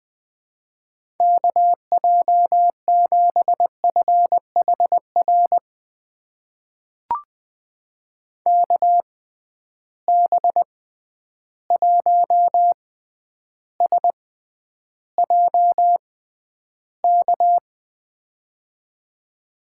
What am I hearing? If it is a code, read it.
Morse code, heard as a keyed tone, about 20 wpm, spaced as for 10 wpm, 700 Hz. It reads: KJ7FHR KB1SJK
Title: USA Call Signs - 6 Characters - 6x Character Spacing - 20wpm